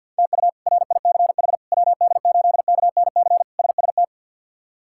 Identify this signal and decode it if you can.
Morse code, heard as a keyed tone, about 51 wpm, 700 Hz. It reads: TU RICH WD8KNC SST
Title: K1USN SST QSO Practice - 50wpm